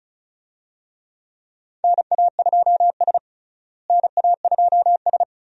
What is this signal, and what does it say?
Morse code, heard as a keyed tone, about 36 wpm, 700 Hz. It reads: NA2S NA2S
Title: USA Call Signs - 4 Characters - Repeated 3x - 35wpm